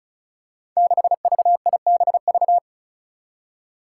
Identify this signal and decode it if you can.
Morse code, heard as a keyed tone, about 36 wpm, 700 Hz. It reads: 6VIBV
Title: Five Challenging Letter-Number Combinations 35wpm